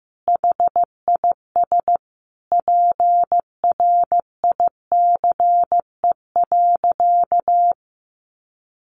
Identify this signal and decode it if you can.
Morse code, heard as a keyed tone, about 15 wpm, 700 Hz. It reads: HIS PRICE.
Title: Sentences from Top 600 Words 15wpm